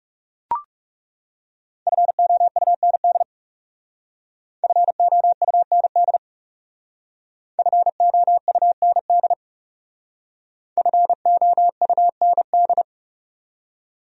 Morse code, heard as a keyed tone, about 37 wpm, 700 Hz. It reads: FOUND FOUND FOUND FOUND
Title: Speed-Racing Top 500 Words 30wpm